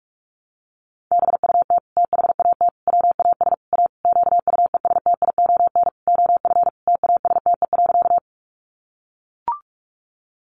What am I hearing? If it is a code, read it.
Morse code, heard as a keyed tone, about 44 wpm, 700 Hz. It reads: BUT THAT WAS A QUESTION OF TASTE.
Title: Sentences from Top 1100 Words 45wpm